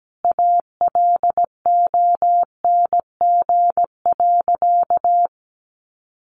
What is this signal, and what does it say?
Morse code, heard as a keyed tone, about 17 wpm, 700 Hz. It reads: ALONG.
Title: Sentences from Top 700 Words 17wpm